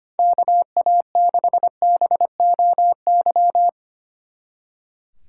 Morse code, heard as a keyed tone, about 27 wpm, 700 Hz. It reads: KA6BOY
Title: Speed Racing - USA Call Signs 25wpm